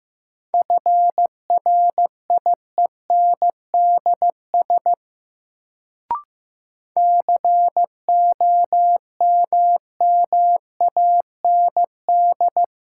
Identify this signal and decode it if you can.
Morse code, heard as a keyed tone, about 15 wpm, 700 Hz. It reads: FRIENDS COMMAND
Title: Sentences from Top 1600 Words - New Word Drill Alt - 15wpm